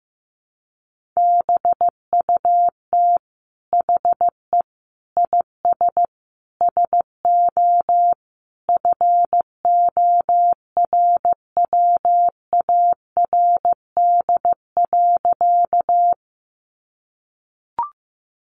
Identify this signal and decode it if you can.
Morse code, heard as a keyed tone, about 15 wpm, 700 Hz. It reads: BUT HE IS SO FORWARD.
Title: Sentences from Top 1300 Words 15wpm